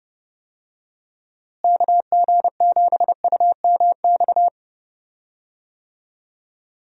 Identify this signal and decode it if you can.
Morse code, heard as a keyed tone, about 30 wpm, 700 Hz. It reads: KG7UMX
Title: USA Call Signs - 6 Characters - Repeated 3x - 30wpm